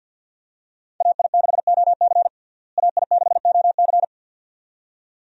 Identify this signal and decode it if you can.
Morse code, heard as a keyed tone, about 52 wpm, 700 Hz. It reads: AI6OC AI6OC
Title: USA Call Signs - 5 Characters - Repeated 2x - 50wpm